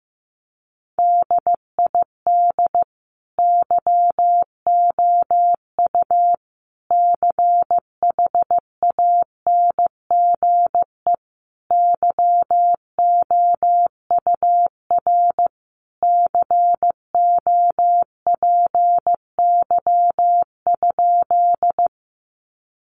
Morse code, heard as a keyed tone, about 15 wpm, 700 Hz. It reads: DID YOU CHANGE YOUR COPY?
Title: Sentences from Top 1400 Words 15wpm